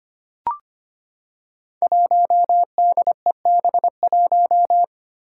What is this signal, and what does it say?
Morse code, heard as a keyed tone, about 25 wpm, 700 Hz. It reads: 1DEB1
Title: Five Challenging Letter-Number Combinations 25wpm